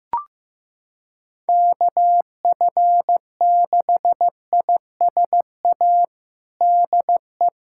Morse code, heard as a keyed tone, about 15 wpm, 700 Hz. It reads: KF6ISA DE
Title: USA Callsign DE Callsign - 6 Characters - 15wpm